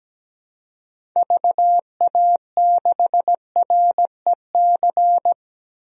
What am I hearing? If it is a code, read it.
Morse code, heard as a keyed tone, about 17 wpm, 700 Hz. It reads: VA6REC
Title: DXCC-100 Call Signs - 6 Characters - 17wpm